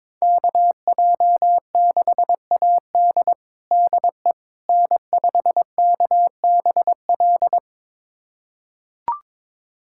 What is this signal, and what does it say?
Morse code, heard as a keyed tone, about 22 wpm, 700 Hz. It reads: KJ6AD DE N5KBL
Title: USA Callsign DE Callsign - 5 Characters - 22wpm